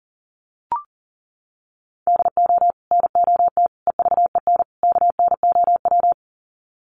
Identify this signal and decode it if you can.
Morse code, heard as a keyed tone, about 40 wpm, 700 Hz. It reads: DO NOT EVEN KNOW
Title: Sets of 4 Words 40wpm